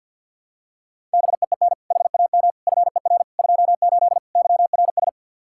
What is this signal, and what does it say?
Morse code, heard as a keyed tone, about 52 wpm, 700 Hz. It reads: BEEN HAM FER 28 YRS
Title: K1USN SST QSO Practice 50wpm